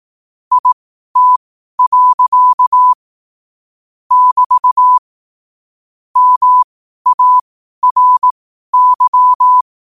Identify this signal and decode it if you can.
Morse code, heard as a keyed tone, about 18 wpm, spaced as for 12 wpm, 1000 Hz. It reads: IT. <BT> MARY
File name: qotdfiles / 12wpm